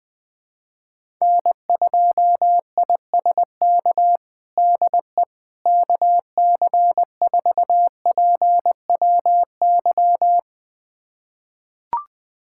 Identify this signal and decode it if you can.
Morse code, heard as a keyed tone, about 20 wpm, 700 Hz. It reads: N2ISK DE KC4PWY
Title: USA Callsign DE Callsign 20wpm